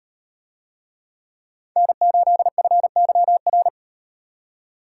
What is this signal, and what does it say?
Morse code, heard as a keyed tone, about 36 wpm, 700 Hz. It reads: N8FYR